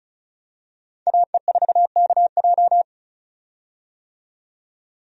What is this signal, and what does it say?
Morse code, heard as a keyed tone, about 36 wpm, 700 Hz. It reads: AE4KJ
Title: USA Call Signs - 5 Characters - 35wpm